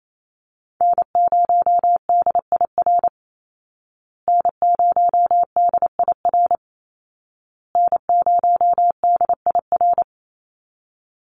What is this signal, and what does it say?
Morse code, heard as a keyed tone, about 28 wpm, 700 Hz. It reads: N0DIR N0DIR N0DIR
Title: Call Signs Repeated 2x - 28wpm